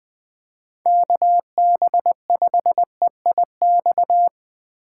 Morse code, heard as a keyed tone, about 20 wpm, 700 Hz. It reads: KB5EIX
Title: Speed Racing - USA 2x3 Call Signs 17wpm